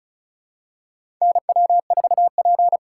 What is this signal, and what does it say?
Morse code, heard as a keyed tone, about 34 wpm, 700 Hz. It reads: NW4P
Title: USA Call Signs - 4 Characters - 3x Character Spacing - 35wpm